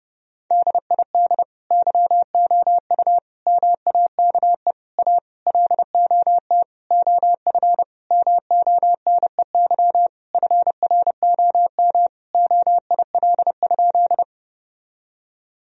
Morse code, heard as a keyed tone, about 30 wpm, 700 Hz. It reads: DID YOU MAKE A LOT OF MONEY FROM OIL?
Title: Sentences from Top 1000 Words - Review Part 4 - 30wpm